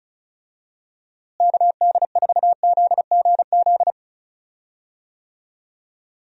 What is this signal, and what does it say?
Morse code, heard as a keyed tone, about 36 wpm, 700 Hz. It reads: KD4ZGZ
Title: USA Call Signs - 6 Characters - Repeated 3x - 35wpm